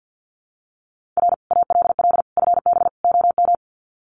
Morse code, heard as a keyed tone, about 50 wpm, 700 Hz. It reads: R ALL FB OM
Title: Common QSO Phrases 50wpm